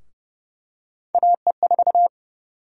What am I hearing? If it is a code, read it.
Morse code, heard as a keyed tone, about 32 wpm, 700 Hz. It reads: AE4
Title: USA Call Sign Prefixes - 30wpm